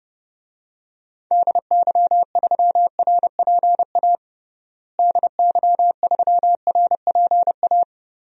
Morse code, heard as a keyed tone, about 30 wpm, 700 Hz. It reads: DY3RPA DY3RPA
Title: DXCC-100 Call Signs - 6 Characters - Repeated 2x - 30wpm